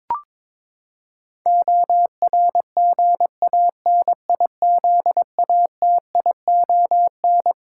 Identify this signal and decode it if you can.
Morse code, heard as a keyed tone, about 22 wpm, 700 Hz. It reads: ORGANIZATION
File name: practice / 22wpm / Twelve Letter Words - 22wpm